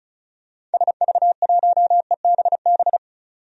Morse code, heard as a keyed tone, about 36 wpm, 700 Hz. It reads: SV1EBB
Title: DXCC-100 Call Signs - 6 Characters - 35wpm